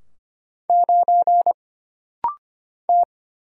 Morse code, heard as a keyed tone, about 25 wpm, 700 Hz. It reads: 9 T
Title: Single Letter-Number - Rapid-Fire 25wpm